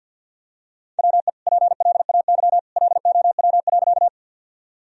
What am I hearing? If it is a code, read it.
Morse code, heard as a keyed tone, about 51 wpm, 700 Hz. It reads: WE PLAY LOW.
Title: Sentences from Top 1000 Words - Review Part 1 - 50wpm